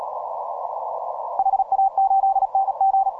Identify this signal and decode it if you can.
Morse code, heard as a keyed tone, about 37 wpm, 780 Hz. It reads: HA8NG